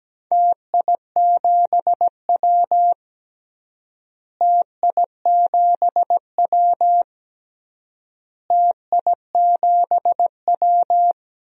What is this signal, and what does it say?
Morse code, heard as a keyed tone, about 17 wpm, 700 Hz. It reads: TI7W TI7W TI7W
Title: DXCC-100 Call Signs - 4 Characters - Repeated 3x - 17wpm